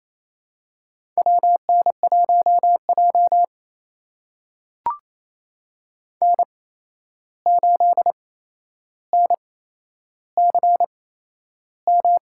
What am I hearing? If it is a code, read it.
Morse code, heard as a keyed tone, about 28 wpm, spaced as for 11 wpm, 700 Hz. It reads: WN1J N8NCM
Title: USA Call Signs - 8x Character Spacing - 28wpm